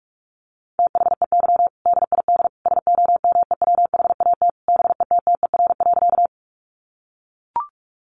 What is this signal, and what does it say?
Morse code, heard as a keyed tone, about 44 wpm, 700 Hz. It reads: THEY DID SOMEWHAT BETTER.
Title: Sentences from Top 1300 Words 45wpm